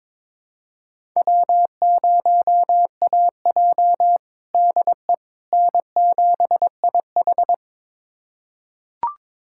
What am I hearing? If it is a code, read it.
Morse code, heard as a keyed tone, about 22 wpm, 700 Hz. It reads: W0AJ DE N7IH
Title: USA Callsign DE Callsign - 4 Characters - 22wpm